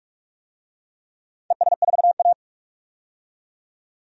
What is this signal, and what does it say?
Morse code, heard as a keyed tone, about 43 wpm, 700 Hz. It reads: ES4A
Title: DXCC-100 Call Signs - 4 Characters - 45wpm